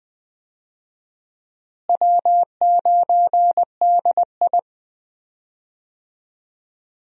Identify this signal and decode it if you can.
Morse code, heard as a keyed tone, about 20 wpm, 700 Hz. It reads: W9DI
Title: USA Call Signs - 4 Characters - Repeated 3x - 20wpm